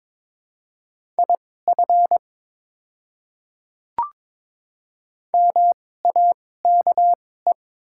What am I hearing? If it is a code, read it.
Morse code, heard as a keyed tone, about 22 wpm, spaced as for 16 wpm, 700 Hz. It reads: IF MAKE
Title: 2x Character Spacing - Top 100 Words - 22wpm